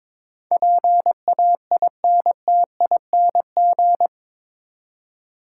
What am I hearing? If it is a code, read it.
Morse code, heard as a keyed tone, about 22 wpm, 700 Hz. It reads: PAINTING
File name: practice / 22wpm / Eight Letter Words - 22wpm